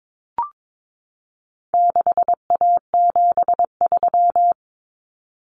Morse code, heard as a keyed tone, about 22 wpm, 700 Hz. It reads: 6A73
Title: Common Cut Numbers - 4-Digit 22wpm